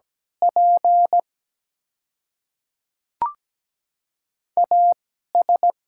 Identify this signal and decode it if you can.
Morse code, heard as a keyed tone, about 17 wpm, spaced as for 12 wpm, 700 Hz. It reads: P AS
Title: 2x Character Spacing - Top 100 Words - 17wpm